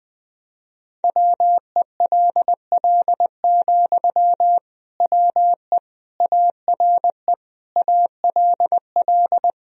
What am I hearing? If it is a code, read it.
Morse code, heard as a keyed tone, about 20 wpm, 700 Hz. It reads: WELL, WE ARE ALL
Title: Sentences from Top 1000 Words - Review Part 4 - 20wpm